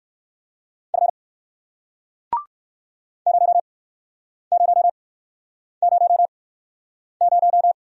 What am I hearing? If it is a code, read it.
Morse code, heard as a keyed tone, about 56 wpm, spaced as for 9 wpm, 700 Hz. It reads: F 0000
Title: Speed-Racing Single Letter-Number 45wpm